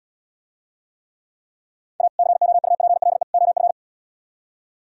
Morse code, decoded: A COUPLE OF

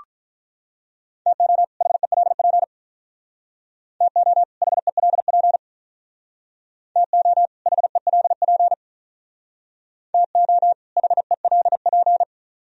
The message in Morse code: TO HELP TO HELP TO HELP TO HELP

